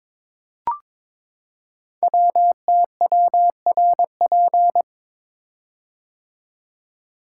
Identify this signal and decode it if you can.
Morse code, heard as a keyed tone, about 22 wpm, 700 Hz. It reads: WTWRP
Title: Code Groups - 5 Letters - 22wpm